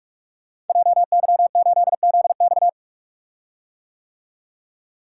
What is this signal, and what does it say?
Morse code, heard as a keyed tone, about 45 wpm, 700 Hz. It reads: JY8ZX
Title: Code Groups - 5 Letter:Numbers - 45wpm